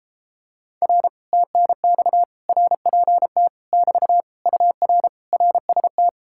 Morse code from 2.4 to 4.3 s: RPT <BT>

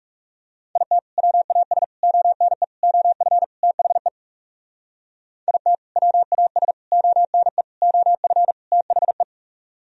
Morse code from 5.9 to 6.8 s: WAS

